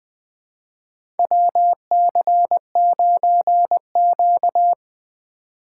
Morse code, heard as a keyed tone, about 20 wpm, 700 Hz. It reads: WC9Q
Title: USA Call Signs - 4 Characters - 4x Character Spacing - 20wpm